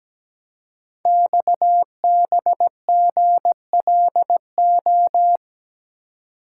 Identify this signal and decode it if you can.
Morse code, heard as a keyed tone, about 17 wpm, 700 Hz. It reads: XBGLO
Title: Code Groups - 5 Letter:Numbers - 17wpm